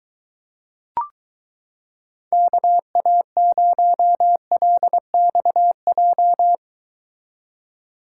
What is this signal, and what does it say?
Morse code, heard as a keyed tone, about 23 wpm, 700 Hz. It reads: KA0LXJ